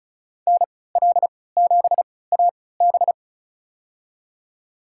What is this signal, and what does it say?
Morse code, heard as a keyed tone, about 34 wpm, spaced as for 20 wpm, 700 Hz. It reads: NL7AB
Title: USA Call Signs - 3x Character Spacing - 35wpm